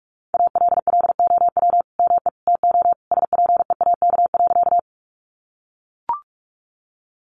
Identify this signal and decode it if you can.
Morse code, heard as a keyed tone, about 44 wpm, 700 Hz. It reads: ALLOW ME TO SPEAK.
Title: Sentences from Top 1100 Words 45wpm